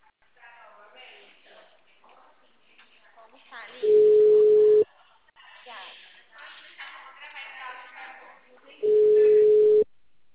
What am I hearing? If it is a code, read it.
Ringback tone (Europe, 425 Hz), with speech.
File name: out- -2044- - - .19787